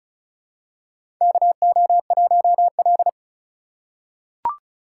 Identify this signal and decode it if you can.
Morse code, heard as a keyed tone, about 34 wpm, 700 Hz. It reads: KO1L
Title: USA Call Signs - 4 Characters - 2x Character Spacing - 35wpm